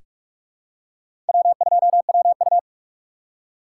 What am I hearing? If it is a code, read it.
Morse code, heard as a keyed tone, about 44 wpm, 700 Hz. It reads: W2WU
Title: USA Call Signs - 3x Character Spacing - 45wpm